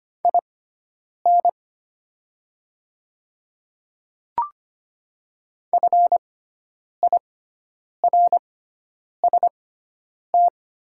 Morse code, heard as a keyed tone, about 25 wpm, spaced as for 9 wpm, 700 Hz. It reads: IN FIRST
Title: 6x Character Spacing - Top 500 Words - 25wpm